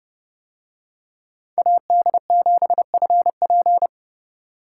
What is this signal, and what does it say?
Morse code, heard as a keyed tone, about 29 wpm, 700 Hz. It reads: AD7FP